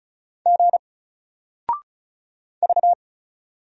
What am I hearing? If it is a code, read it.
Morse code, heard as a keyed tone, about 35 wpm, 700 Hz. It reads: G V
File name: Single-Letter-Number-35wpm